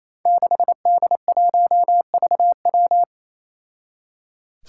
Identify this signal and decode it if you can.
Morse code, heard as a keyed tone, about 28 wpm, 700 Hz. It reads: 6D1VW
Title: Five Challenging Letter-Number Combinations 28wpm